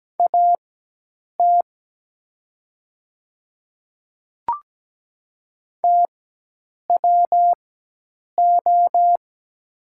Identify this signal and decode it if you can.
Morse code, heard as a keyed tone, about 17 wpm, spaced as for 8 wpm, 700 Hz. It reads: AT TWO
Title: 4x Character Spacing - Top 100 Words - 17wpm